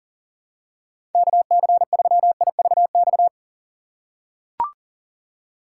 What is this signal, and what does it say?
Morse code, heard as a keyed tone, about 40 wpm, 700 Hz. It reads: KC3IVX
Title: USA Call Signs - 6 Characters - 3x Character Spacing - 40wpm